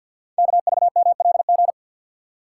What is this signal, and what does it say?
Morse code, heard as a keyed tone, about 48 wpm, 700 Hz. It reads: KVMLG
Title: Code Groups - 5 Letters - 50wpm